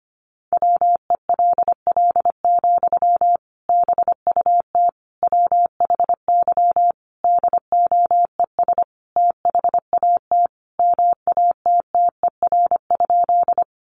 Morse code, heard as a keyed tone, about 25 wpm, 700 Hz. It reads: WELL, BUT WHY DOES THAT MATTER?
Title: Sentences from Top 700 Words 25wpm